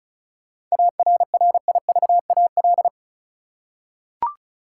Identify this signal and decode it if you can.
Morse code, heard as a keyed tone, about 34 wpm, 700 Hz. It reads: ARRIVAL
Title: Seven Letter Words - 35wpm